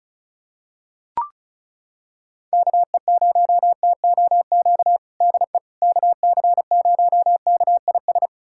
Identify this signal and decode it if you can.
Morse code, heard as a keyed tone, about 36 wpm, 700 Hz. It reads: KE0TOQ DE KC0KIS
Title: USA Callsign DE Callsign - 6 Characters - 35wpm